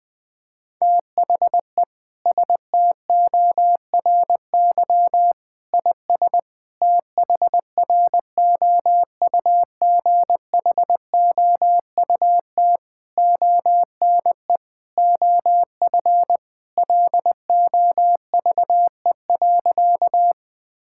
Morse code, heard as a keyed tone, about 20 wpm, 700 Hz. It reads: THE STORY IS THROUGHOUT ONE OF LOVE.
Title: Sentences from Top 1000 Words - Review Part 5 - 20wpm